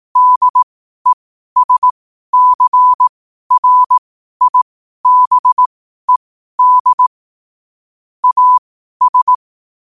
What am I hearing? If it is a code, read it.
Morse code, heard as a keyed tone, about 18 wpm, spaced as for 12 wpm, 1000 Hz. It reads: DESCRIBED AS
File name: qotdfiles / 12wpm